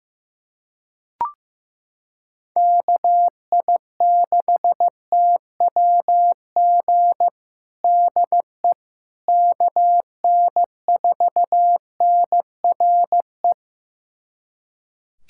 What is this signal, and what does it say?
Morse code, heard as a keyed tone, about 16 wpm, 700 Hz. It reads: KI6TWG DE KN4NRE
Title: USA Callsign DE Callsign - 6 Characters - 15wpm